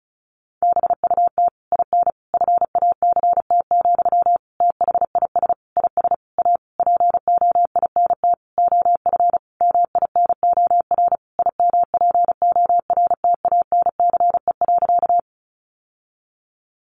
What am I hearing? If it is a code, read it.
Morse code, heard as a keyed tone, about 34 wpm, 700 Hz. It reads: BUT IN FACT, THIS IS A POINT OF MINOR IMPORTANCE.
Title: Sentences from Top 1500 Words 35wpm